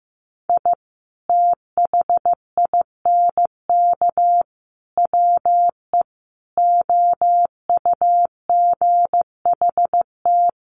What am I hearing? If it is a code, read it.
Morse code, heard as a keyed tone, about 15 wpm, 700 Hz. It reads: I THINK WE OUGHT
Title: Sets of 5 Words Encore 15wpm